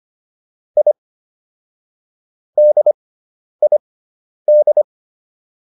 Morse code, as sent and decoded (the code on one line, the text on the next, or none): .. / -.. .. -..
I DID